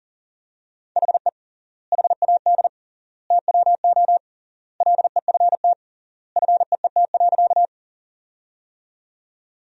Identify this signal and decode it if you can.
Morse code, heard as a keyed tone, about 40 wpm, 700 Hz. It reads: HE HAD TWO LEFT FEET.
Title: 3x Word-Spacing - Sentences from Top 300 Words - 40wpm